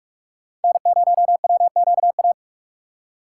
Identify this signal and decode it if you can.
Morse code, heard as a keyed tone, about 57 wpm, 700 Hz. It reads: N0WQA